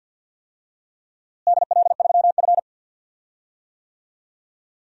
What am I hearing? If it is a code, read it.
Morse code, heard as a keyed tone, about 52 wpm, 700 Hz. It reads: DL3F
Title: DXCC-100 Call Signs - 4 Characters - Repeated 2x - 50wpm